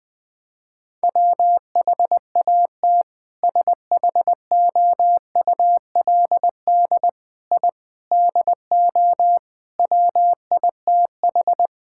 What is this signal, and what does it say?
Morse code, heard as a keyed tone, about 20 wpm, 700 Hz. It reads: WHAT SHOULD I DO WITH
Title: Sentences with 4-Word Phrases - Spoken Beginning - 20wpm